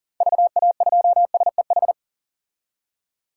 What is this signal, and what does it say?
Morse code, heard as a keyed tone, about 40 wpm, 700 Hz. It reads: VA2SEH
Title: DXCC-100 Call Signs - 6 Characters - 40wpm